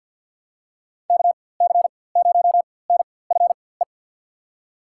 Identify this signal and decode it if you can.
Morse code, heard as a keyed tone, about 52 wpm, spaced as for 24 wpm, 700 Hz. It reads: KC0NFE
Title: USA Call Signs - 4x Character Spacing - 50wpm